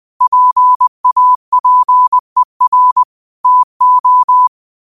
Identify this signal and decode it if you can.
Morse code, heard as a keyed tone, about 20 wpm, 1000 Hz. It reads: PAPER TO